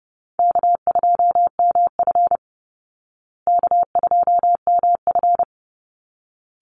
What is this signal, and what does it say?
Morse code, heard as a keyed tone, about 30 wpm, 700 Hz. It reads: K2MF K2MF
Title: Easy Call Signs Repeated 2x - 30wpm